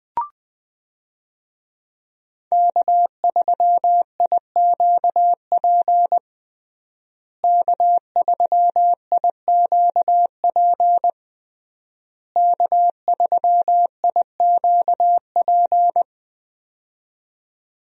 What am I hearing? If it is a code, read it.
Morse code, heard as a keyed tone, about 20 wpm, 700 Hz. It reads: K3IQP K3IQP K3IQP
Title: USA Call Signs - 5 Characters - Repeated 3x - 20wpm